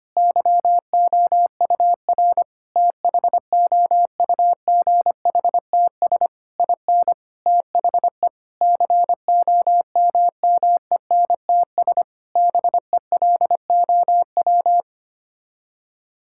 Morse code, heard as a keyed tone, about 25 wpm, 700 Hz. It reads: YOUR THOUGHTS IN THE COMMENTS BELOW
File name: Sets of 9 Words - 25wpm